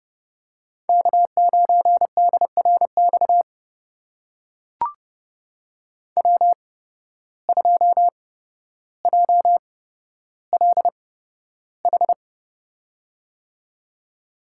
Morse code, heard as a keyed tone, about 31 wpm, spaced as for 13 wpm, 700 Hz. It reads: K9DRX W2JLH
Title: USA Call Signs - 5 Characters - 8x Character Spacing - 30wpm